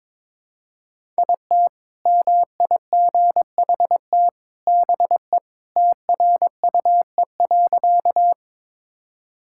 Morse code, heard as a keyed tone, about 22 wpm, 700 Hz. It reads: IT MIGHT BE TRUE.
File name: practice / 22wpm / Sentences with 2-Word Phrases - 22wpm